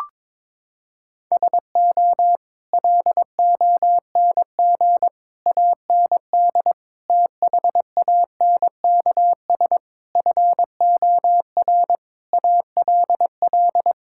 Morse code, heard as a keyed tone, about 22 wpm, 700 Hz. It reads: SO LONG AND THANKS FOR ALL
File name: practice / 22wpm / Sets of 8 Words - 22wpm